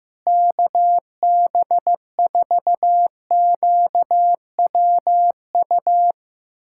Morse code, heard as a keyed tone, about 14 wpm, 700 Hz. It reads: KB4QWU
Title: USA Callsign DE Callsign - 6 Characters - 15wpm